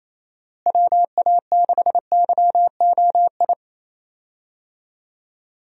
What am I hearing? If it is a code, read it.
Morse code, heard as a keyed tone, about 28 wpm, 700 Hz. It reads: WA6YOI
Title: USA Call Signs - 6 Characters - Repeated 3x - 28wpm